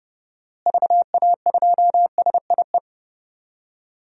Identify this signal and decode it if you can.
Morse code, heard as a keyed tone, about 30 wpm, 700 Hz. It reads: VA2SIE
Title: DXCC-100 Call Signs - 6 Characters - Repeated 3x - 30wpm